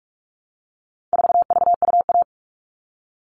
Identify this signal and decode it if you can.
Morse code, heard as a keyed tone, about 44 wpm, 700 Hz. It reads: 4VUA
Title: All Cut Numbers - 4-Digit 45wpm